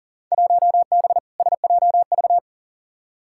Morse code, heard as a keyed tone, about 40 wpm, 700 Hz. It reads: 1B SJV
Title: Field Day Exchanges - 40wpm